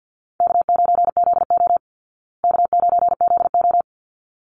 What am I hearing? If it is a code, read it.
Morse code, heard as a keyed tone, about 50 wpm, 700 Hz. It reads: K9ZO K9ZO
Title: Easy Call Signs Repeated 2x - 50wpm